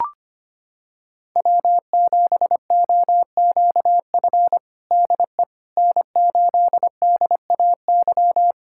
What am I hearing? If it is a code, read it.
Morse code, heard as a keyed tone, about 25 wpm, 700 Hz. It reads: W7OQF DE N8DAY
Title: USA Callsign DE Callsign - 5 Characters - 25wpm